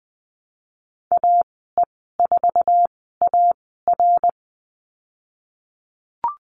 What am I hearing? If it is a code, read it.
Morse code, heard as a keyed tone, about 20 wpm, spaced as for 15 wpm, 700 Hz. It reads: AE4AR
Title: 122 Lesson 20g - Number 3 - Call Signs 20wpm